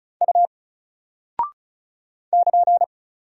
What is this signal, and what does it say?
Morse code, heard as a keyed tone, about 34 wpm, 700 Hz. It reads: U <KN>
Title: Single Character (Letter:Number:Punctuation:Prosign) Alt 35wpm